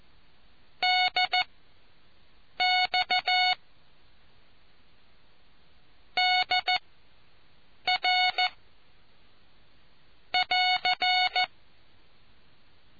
Morse code, heard as a keyed tone, about 14 wpm, spaced as for 6 wpm, 730 Hz. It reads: DX DR <AR>